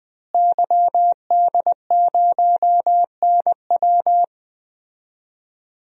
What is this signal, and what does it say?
Morse code, heard as a keyed tone, about 20 wpm, 700 Hz. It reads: YD0NW
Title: DXCC-100 Call Signs - 5 Characters - Repeated 3x - 20wpm